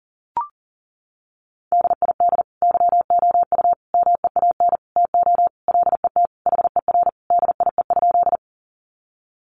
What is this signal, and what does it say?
Morse code, heard as a keyed tone, about 40 wpm, 700 Hz. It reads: DID YOU MEAN TO LET HER DIE?
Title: Sentences from Top 1300 Words 40wpm